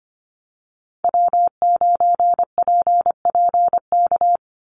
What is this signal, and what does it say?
Morse code, heard as a keyed tone, about 25 wpm, 700 Hz. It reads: W9PPK
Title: USA Call Signs 25wpm